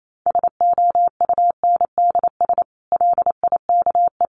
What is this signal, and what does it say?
Morse code, heard as a keyed tone, about 28 wpm, 700 Hz. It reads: SOUNDS LIKE
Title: A Sound Of Thunder 28wpm